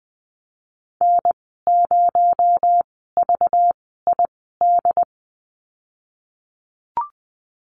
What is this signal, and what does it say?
Morse code, heard as a keyed tone, about 20 wpm, spaced as for 15 wpm, 700 Hz. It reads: N0VID
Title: 238 Lesson 38g - Letter q - Call Signs 20wpm